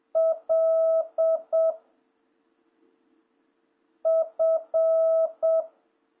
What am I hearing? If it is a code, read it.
Morse code, heard as a keyed tone, about 7 wpm, 660 Hz. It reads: L F
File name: jh8cbh / 2026cw